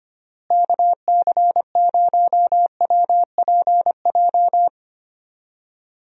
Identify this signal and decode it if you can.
Morse code, heard as a keyed tone, about 25 wpm, 700 Hz. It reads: KC0WPJ
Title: USA Call Signs - 6 Characters - Repeated 2x - 25wpm